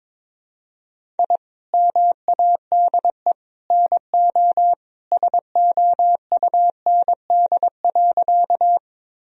Morse code, heard as a keyed tone, about 22 wpm, 700 Hz. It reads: I MADE NO SOUND.
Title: Sentences from Top 1000 Words - Review Part 1 - 22wpm